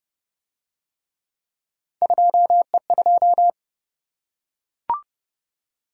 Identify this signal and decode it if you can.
Morse code, heard as a keyed tone, about 27 wpm, 700 Hz. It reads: 2E2